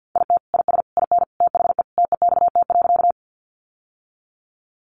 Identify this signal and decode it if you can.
Morse code, heard as a keyed tone, about 50 wpm, 700 Hz. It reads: IT IS IN THE TEXT.
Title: Sentences from Top 1200 Words 50wpm